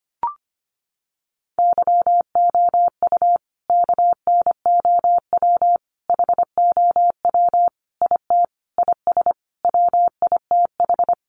Morse code, heard as a keyed tone, about 25 wpm, 700 Hz. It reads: YOU KNOW HOW IT IS WITH
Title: Sentences from Top 100 Words 25wpm